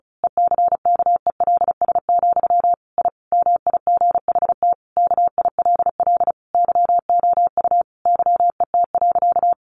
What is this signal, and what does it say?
Morse code, heard as a keyed tone, about 34 wpm, 700 Hz. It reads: ECKELS, I MIGHT KILL YOU YET.
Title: A Sound Of Thunder 35wpm